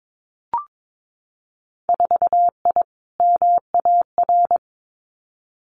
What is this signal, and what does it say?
Morse code, heard as a keyed tone, about 22 wpm, 700 Hz. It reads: 4I MAR
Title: Winter Field Day Exchanges Encore 22wpm